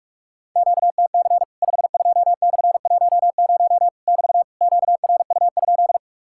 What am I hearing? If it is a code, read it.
Morse code, heard as a keyed tone, about 44 wpm, 700 Hz. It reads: QTC 52/10 <BT> QRU?
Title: Worked All Europe DX Contest - QTC Exchange - 45wpm